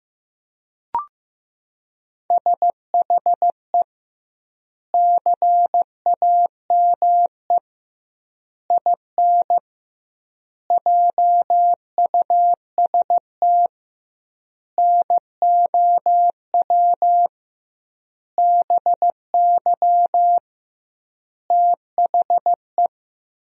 Morse code, heard as a keyed tone, about 15 wpm, 700 Hz. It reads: SHE CAME IN JUST NOW BY THE
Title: 2x Word-Spacing - Sentences from Top 300 Words - 15wpm